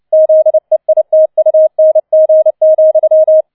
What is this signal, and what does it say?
Morse code, heard as a keyed tone, about 29 wpm, 630 Hz. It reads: ZEITUNG,